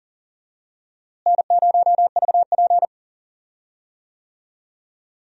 Morse code, heard as a keyed tone, about 40 wpm, 700 Hz. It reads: N0VP
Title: USA Call Signs - 4 Characters - Repeated 2x - 40wpm